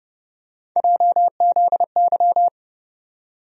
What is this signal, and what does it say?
Morse code, heard as a keyed tone, about 30 wpm, 700 Hz. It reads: JZY
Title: USA Call Sign Postfixes - 3 Letters 30wpm